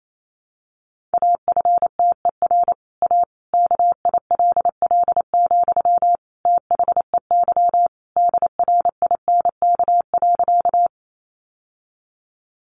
Morse code, heard as a keyed tone, about 28 wpm, 700 Hz. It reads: AFTER A KILL, THEY DRINK.
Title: Sentences from Top 1500 Words 28wpm